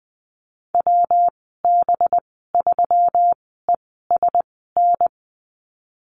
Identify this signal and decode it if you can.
Morse code, heard as a keyed tone, about 19 wpm, spaced as for 14 wpm, 700 Hz. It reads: WB3ESN
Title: 200 Lesson 32g - Letter k - Call Signs 20wpm